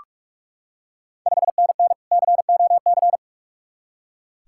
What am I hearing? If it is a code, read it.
Morse code, heard as a keyed tone, about 45 wpm, 700 Hz. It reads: 5NN COC
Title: Ontario QSO Party Exchange 45wpm